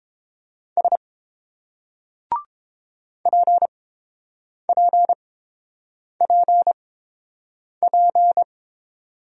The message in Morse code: S PPPP

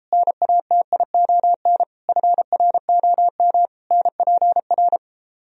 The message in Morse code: NATION FROM NPR

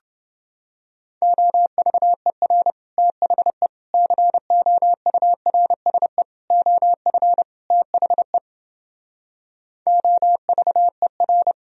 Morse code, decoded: OVER THE COURSE OF THE OVER

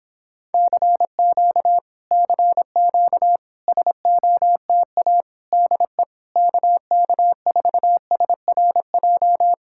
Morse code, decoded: CQ CQ SOTA DE KK4SRJ

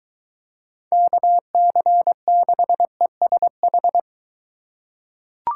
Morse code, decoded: KC6ESH

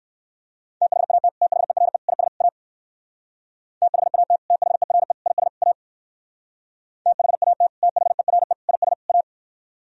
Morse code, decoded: THAT THERE IS A THAT THERE IS A THAT THERE IS A

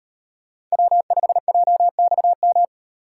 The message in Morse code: W5JXM

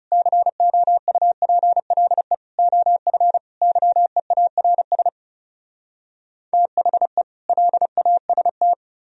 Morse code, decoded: COUPLE OF YEARS THE LAST

